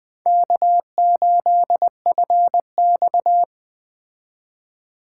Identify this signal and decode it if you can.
Morse code, heard as a keyed tone, about 20 wpm, 700 Hz. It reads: K8FX
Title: Speed Racing - USA 1x2 Call Signs 17wpm